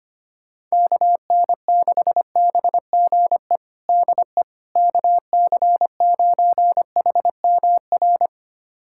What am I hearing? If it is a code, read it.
Morse code, heard as a keyed tone, about 25 wpm, 700 Hz. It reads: KN6BGE DE KC9HMR
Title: USA Callsign DE Callsign 25wpm